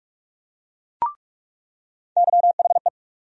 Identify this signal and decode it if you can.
Morse code, heard as a keyed tone, about 47 wpm, 700 Hz. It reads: YHE
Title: USA Call Sign Postfixes - 3 Letters 45wpm